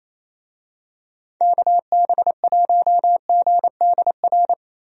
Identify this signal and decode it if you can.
Morse code, heard as a keyed tone, about 27 wpm, 700 Hz. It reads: KB1GDR
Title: USA Call Signs - 6 Characters - 28wpm